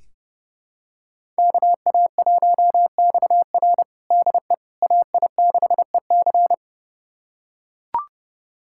Morse code, heard as a keyed tone, about 30 wpm, 700 Hz. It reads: KA1XR DE AI6EC
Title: USA Callsign DE Callsign - 5 Characters - 30wpm